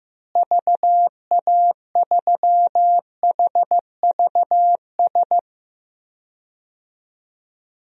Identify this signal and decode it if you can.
Morse code, heard as a keyed tone, about 15 wpm, 700 Hz. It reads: VA3HVS
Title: DXCC-100 Call Signs - 6 Characters - Repeated 2x - 15wpm